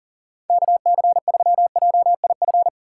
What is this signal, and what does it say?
Morse code, heard as a keyed tone, about 37 wpm, 700 Hz. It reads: KC3JIF
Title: USA Call Signs - 6 Characters - 3x Character Spacing - 40wpm